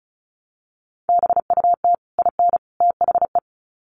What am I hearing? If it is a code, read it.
Morse code, heard as a keyed tone, about 34 wpm, 700 Hz. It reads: BUT IN THE
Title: Sets of 3 Words 35wpm